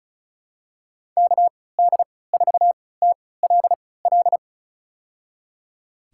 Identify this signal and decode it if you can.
Morse code, heard as a keyed tone, about 34 wpm, spaced as for 20 wpm, 700 Hz. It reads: KD4TLL
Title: USA Call Signs - 6 Characters - 3x Character Spacing - 35wpm